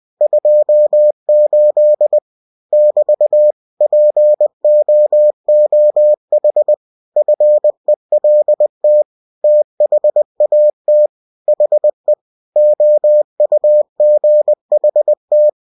Morse code, heard as a keyed tone, about 20 wpm, 600 Hz. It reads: 28 <BT> POOH FELT THAT HE OUGHT